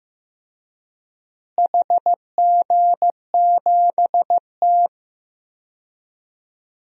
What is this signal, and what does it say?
Morse code, heard as a keyed tone, about 15 wpm, 700 Hz. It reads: HG7T
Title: DXCC-100 Call Signs - 4 Characters - Repeated 2x - 15wpm